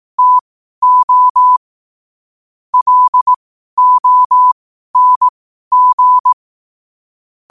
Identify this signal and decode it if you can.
Morse code, heard as a keyed tone, about 18 wpm, spaced as for 12 wpm, 1000 Hz. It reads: TO LONG
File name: qotdfiles / 12wpm